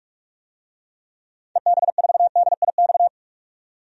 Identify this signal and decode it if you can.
Morse code, heard as a keyed tone, about 45 wpm, 700 Hz. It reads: EB4DIX
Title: DXCC-100 Call Signs - 6 Characters - 45wpm